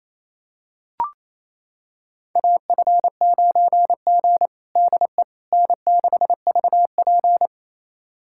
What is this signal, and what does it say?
Morse code, heard as a keyed tone, about 28 wpm, 700 Hz. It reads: AF9G DE N6VP
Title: USA Callsign DE Callsign - 4 Characters - 28wpm